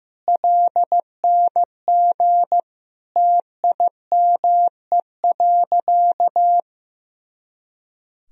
Morse code, heard as a keyed tone, about 15 wpm, 700 Hz. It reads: LNG TIME.
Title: Sentences from Top 100 Words and Abbreviations 15wpm